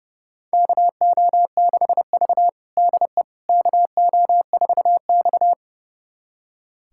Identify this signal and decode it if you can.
Morse code, heard as a keyed tone, about 30 wpm, 700 Hz. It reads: KO6V DE KO4X
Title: USA Callsign DE Callsign - 4 Characters - 30wpm